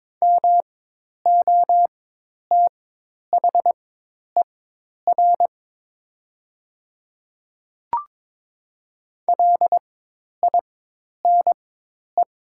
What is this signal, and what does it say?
Morse code, heard as a keyed tone, about 22 wpm, spaced as for 10 wpm, 700 Hz. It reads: MOTHER LINE
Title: 4x Character Spacing - Top 200 Words - 22wpm